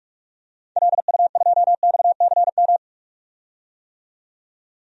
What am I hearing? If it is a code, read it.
Morse code, heard as a keyed tone, about 47 wpm, 700 Hz. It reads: LU2XCM
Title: DXCC-100 Call Signs - 6 Characters - 45wpm